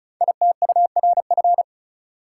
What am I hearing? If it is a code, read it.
Morse code, heard as a keyed tone, about 36 wpm, 690 Hz. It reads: ITURF
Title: Code Groups - 5 Letters - 35wpm